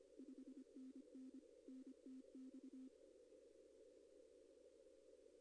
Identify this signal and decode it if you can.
Morse code, heard as a keyed tone, about 25 wpm, 260 Hz. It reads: 5NN NTX
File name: contests / arrl-160-k5tr-op-w5tn -2200